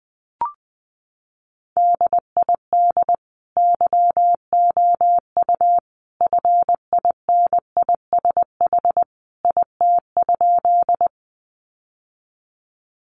Morse code, heard as a keyed tone, about 19 wpm, 700 Hz. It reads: DID YOU FINISH IT?
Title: Sentences from Top 1200 Words 20wpm